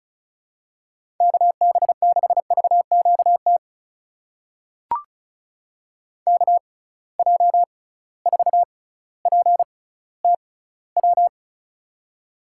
Morse code, heard as a keyed tone, about 36 wpm, spaced as for 18 wpm, 700 Hz. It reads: KB6VQT KJ4PTW
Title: USA Call Signs - 6 Characters - 6x Character Spacing - 35wpm